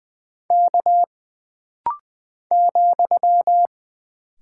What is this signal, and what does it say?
Morse code, heard as a keyed tone, about 20 wpm, 700 Hz. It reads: K ,
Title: Single Character (Letter:Number:Punctuation:Prosign) - Rapid-Fire - 20wpm